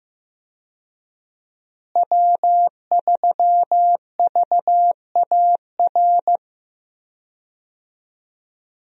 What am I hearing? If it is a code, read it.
Morse code, heard as a keyed tone, about 15 wpm, 700 Hz. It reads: W3VAR
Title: USA Call Signs - 5 Characters - Repeated 3x - 15wpm